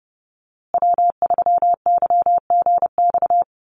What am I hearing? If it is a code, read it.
Morse code, heard as a keyed tone, about 30 wpm, 700 Hz. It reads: W3YGX
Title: USA Call Signs Repeated 3x 30wpm